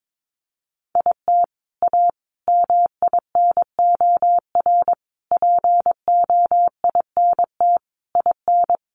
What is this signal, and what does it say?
Morse code, heard as a keyed tone, about 22 wpm, 700 Hz. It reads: IT A MINOR POINT IN
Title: Sentences from Top 1500 Words 22wpm